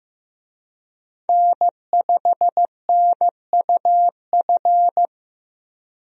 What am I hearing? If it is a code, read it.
Morse code, heard as a keyed tone, about 15 wpm, 700 Hz. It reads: N5NUF
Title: USA Call Signs - 5 Characters - Repeated 3x - 15wpm